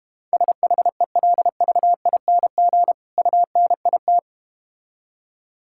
Morse code, heard as a keyed tone, about 31 wpm, 700 Hz. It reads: SHELVING UNIT